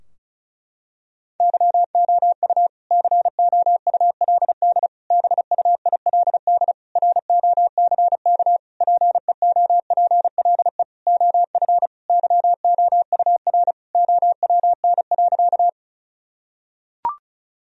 Morse code, heard as a keyed tone, about 34 wpm, 700 Hz. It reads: YOU COULD BUILD ROCK PEOPLE OF YOUR OWN.
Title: Sentences from Top 1000 Words - Review Part 2 - 35wpm